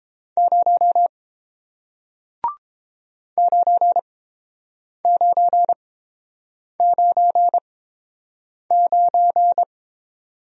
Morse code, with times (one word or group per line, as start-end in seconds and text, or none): 0.4-1.1 0
3.4-9.6 9999